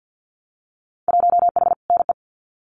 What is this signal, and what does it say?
Morse code, heard as a keyed tone, about 50 wpm, 700 Hz. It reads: 1H NE
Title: Winter Field Day Exchanges Encore 50wpm